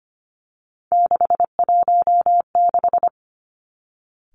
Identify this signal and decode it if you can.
Morse code, heard as a keyed tone, about 25 wpm, 700 Hz. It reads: 616
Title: Three Challenging Letter-Number Combinations 25wpm